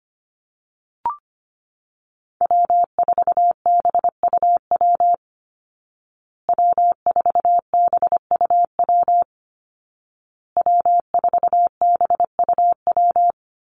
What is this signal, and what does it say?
Morse code, heard as a keyed tone, about 25 wpm, 700 Hz. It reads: W4BUW W4BUW W4BUW
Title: Call Signs Repeated 2x - 25wpm